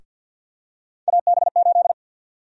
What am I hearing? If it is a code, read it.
Morse code, heard as a keyed tone, about 53 wpm, 700 Hz. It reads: AB8
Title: USA Call Sign Prefixes - Repeated 2x - 50wpm